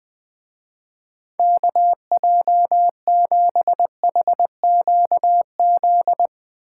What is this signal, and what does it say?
Morse code, heard as a keyed tone, about 24 wpm, 700 Hz. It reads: KJ7HQZ
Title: Speed Racing - USA 2x3 Call Signs 20wpm